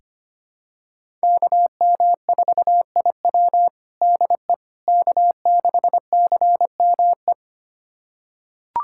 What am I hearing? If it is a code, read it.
Morse code, heard as a keyed tone, about 25 wpm, 700 Hz. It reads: KM4IW DE K6CME
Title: USA Callsign DE Callsign - 5 Characters - 25wpm